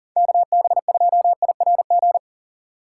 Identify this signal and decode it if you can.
Morse code, heard as a keyed tone, about 53 wpm, 700 Hz. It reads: KB2IRG